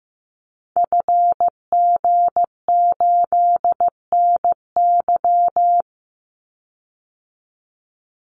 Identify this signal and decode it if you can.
Morse code, heard as a keyed tone, about 15 wpm, 700 Hz. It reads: FG8NY
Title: Call Signs Repeated 2x - 15wpm